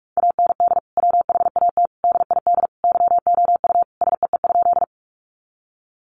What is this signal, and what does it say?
Morse code, heard as a keyed tone, about 44 wpm, 700 Hz. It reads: AND WHAT DID YOU SEE?
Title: Sentences from Top 100 Words 45wpm